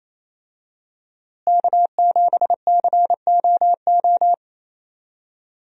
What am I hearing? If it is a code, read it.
Morse code, heard as a keyed tone, about 27 wpm, 700 Hz. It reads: K7COO
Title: USA Call Signs - 5 Characters - Repeated 3x - 28wpm